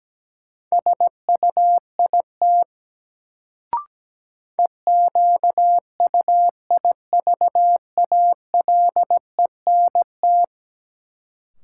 Morse code, heard as a keyed tone, about 17 wpm, 700 Hz. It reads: SUIT EQUIVALENT
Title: Sentences from Top 1600 Words - New Word Drill - 17wpm